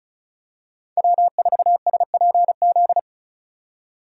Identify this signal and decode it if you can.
Morse code, heard as a keyed tone, about 36 wpm, 700 Hz. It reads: W4SPZ
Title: USA Call Signs - 3x Character Spacing - 35wpm